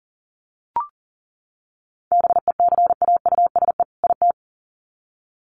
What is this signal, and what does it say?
Morse code, heard as a keyed tone, about 40 wpm, 700 Hz. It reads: BECAUSE IT
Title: Sets of 2 Words 40wpm